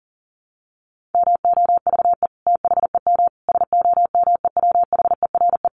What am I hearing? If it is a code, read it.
Morse code, heard as a keyed tone, about 40 wpm, 700 Hz. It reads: MOVE THEM SOMEWHERE
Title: Sentences from Top 1400 Words 40wpm